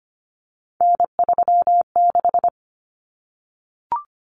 Morse code, heard as a keyed tone, about 25 wpm, 700 Hz. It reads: N36
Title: Common Cut Numbers - 3-Digit 25wpm